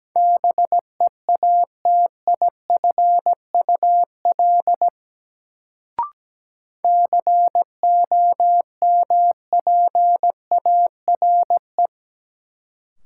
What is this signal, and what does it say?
Morse code, heard as a keyed tone, about 17 wpm, 700 Hz. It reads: BEATIFUL COMPARE
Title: Sentences from Top 1300 Words - New Word Drill - 17wpm